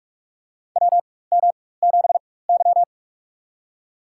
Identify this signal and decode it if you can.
Morse code, heard as a keyed tone, about 44 wpm, spaced as for 21 wpm, 690 Hz. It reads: WM7Y
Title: USA Call Signs - 4x Character Spacing - 45wpm